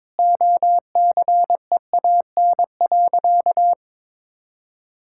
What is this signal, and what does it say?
Morse code, heard as a keyed tone, about 22 wpm, 700 Hz. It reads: OCEAN.
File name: practice / 22wpm / Sentences with 8-Word Phrases - 22wpm